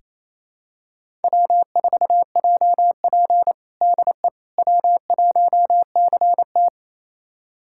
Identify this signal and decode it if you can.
Morse code, heard as a keyed tone, about 28 wpm, 700 Hz. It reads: W4JP DE W1CT
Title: USA Callsign DE Callsign - 4 Characters - 28wpm